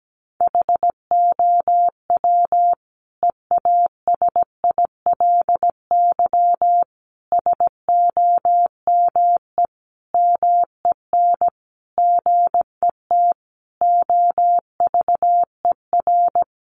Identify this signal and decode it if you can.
Morse code, heard as a keyed tone, about 17 wpm, 700 Hz. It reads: HOW EASILY SOME MEN GET OVER
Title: Sentences from Top 900 Words 17wpm